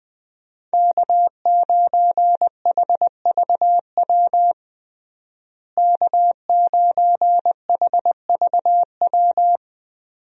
Morse code, heard as a keyed tone, about 20 wpm, 700 Hz. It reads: K9HVW K9HVW
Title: USA Call Signs - 5 Characters - Repeated 2x - 20wpm